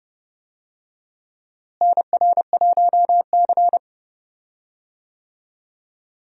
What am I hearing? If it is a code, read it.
Morse code, heard as a keyed tone, about 31 wpm, 700 Hz. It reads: NR1C
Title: USA Call Signs - 4 Characters - Repeated 3x - 30wpm